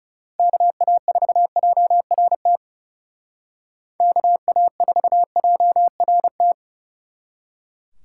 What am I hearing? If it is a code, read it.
Morse code, heard as a keyed tone, about 36 wpm, 700 Hz. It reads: KA4JRT KA4JRT
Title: Speed Racing - USA 2x3 Call Signs 30wpm